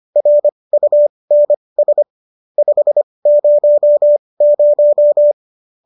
Morse code, heard as a keyed tone, about 25 wpm, spaced as for 20 wpm, 590 Hz. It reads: RUNS 500